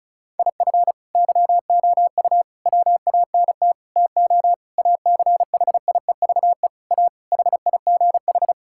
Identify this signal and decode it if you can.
Morse code, heard as a keyed tone, about 34 wpm, 700 Hz. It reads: IF YOU WANT TO ACHIEVE A HIGH
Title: Sentences from Top 1000 Words - Review Part 6 - 35wpm